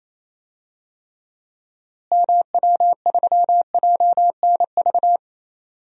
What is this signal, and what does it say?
Morse code, heard as a keyed tone, about 28 wpm, 700 Hz. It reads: MW3JNV
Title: DXCC-100 Call Signs - 6 Characters - Repeated 3x - 28wpm